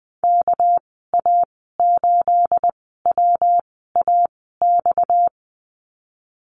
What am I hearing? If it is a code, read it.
Morse code, heard as a keyed tone, about 20 wpm, spaced as for 14 wpm, 700 Hz. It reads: KA8WAX
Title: 238 Lesson 38g - Letter q - Call Signs 20wpm